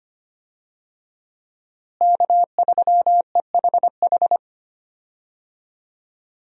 Morse code, heard as a keyed tone, about 25 wpm, 700 Hz. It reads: K3EHH
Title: USA Call Signs - 5 Characters - Repeated 3x - 25wpm